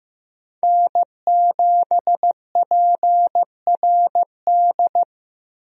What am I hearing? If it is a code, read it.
Morse code, heard as a keyed tone, about 15 wpm, 700 Hz. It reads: N7PRD
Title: USA Call Signs - 5 Characters - Repeated 2x - 15wpm